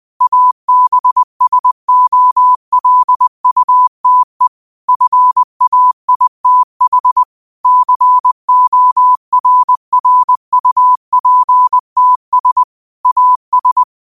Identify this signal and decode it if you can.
Morse code, heard as a keyed tone, about 20 wpm, 1000 Hz. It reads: ABSOLUTE FAITH CORRUPTS AS
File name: qotdfiles / 20wpm